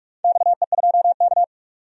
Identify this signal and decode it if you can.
Morse code, heard as a keyed tone, about 44 wpm, 700 Hz. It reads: XE2K
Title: DXCC-100 Call Signs - 4 Characters - 45wpm